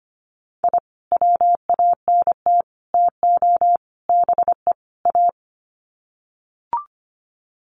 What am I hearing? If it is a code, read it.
Morse code, heard as a keyed tone, about 25 wpm, 700 Hz. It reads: I WANT TO BE A
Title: Sets of 5 Words Encore 25wpm